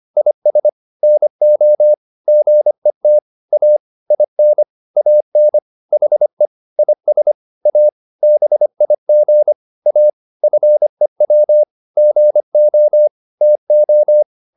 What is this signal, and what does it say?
Morse code, heard as a keyed tone, about 25 wpm, 600 Hz. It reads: IS NO GET A IN AN HE IS A BIG A FEW GO TO